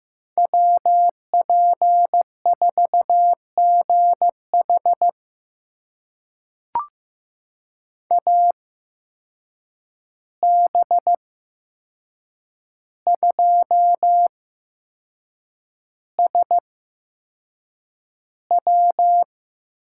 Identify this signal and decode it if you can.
Morse code, heard as a keyed tone, about 15 wpm, spaced as for 6 wpm, 700 Hz. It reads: WP4GH AB2SW
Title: USA Call Signs - 5 Characters - 8x Character Spacing - 15wpm